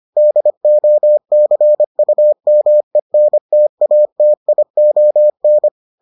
Morse code, heard as a keyed tone, about 25 wpm, 600 Hz. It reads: DOCUMENTATION